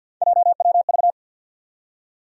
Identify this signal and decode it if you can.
Morse code, heard as a keyed tone, about 46 wpm, 700 Hz. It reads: JWV
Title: USA Call Sign Postfixes - 3 Letters 50wpm